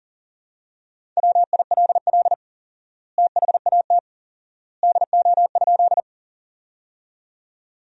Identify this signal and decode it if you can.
Morse code, heard as a keyed tone, about 40 wpm, 700 Hz. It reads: WILL THAT DO?
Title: 4x Word-Spacing - Sentences from Top 100 Words - 40wpm